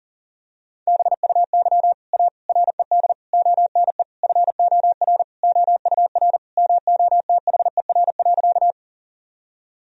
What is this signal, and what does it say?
Morse code, heard as a keyed tone, about 40 wpm, 700 Hz. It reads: BUY A RED ONE FOR OUR MOTHER.
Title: Sentences from Top 1000 Words - Review Part 2 - 40wpm